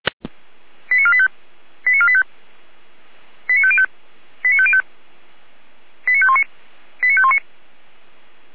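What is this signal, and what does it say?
Five-tone selective call (ZVEI), heard as quick runs of five tones, tones of about 70 ms. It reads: 89485 89485 89505 89505 89519 89519
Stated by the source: Five-tone ZVEI selective call 89001, received by SDR, from feuerwehr-orlamuende fire brigade.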